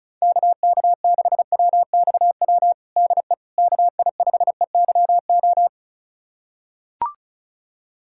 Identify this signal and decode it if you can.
Morse code, heard as a keyed tone, about 36 wpm, 690 Hz. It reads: KK6WXW DE KI5EYO
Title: USA Callsign DE Callsign - 6 Characters - 35wpm